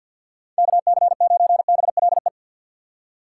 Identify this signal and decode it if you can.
Morse code, heard as a keyed tone, about 50 wpm, 700 Hz. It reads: KC9BLE
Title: USA Call Signs - 6 Characters - 6x Character Spacing - 50wpm